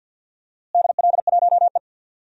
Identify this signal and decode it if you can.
Morse code, heard as a keyed tone, about 51 wpm, 700 Hz. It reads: DL1E
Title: DXCC-100 Call Signs - 4 Characters - Repeated 2x - 50wpm